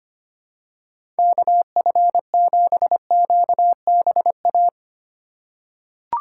Morse code, heard as a keyed tone, about 25 wpm, 700 Hz. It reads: KF7QBA
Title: USA Call Signs - 6 Characters - 4x Character Spacing - 25wpm